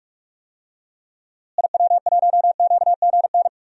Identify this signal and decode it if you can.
Morse code, heard as a keyed tone, about 44 wpm, 700 Hz. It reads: IW1QGN
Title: DXCC-100 Call Signs - 6 Characters - Repeated 2x - 45wpm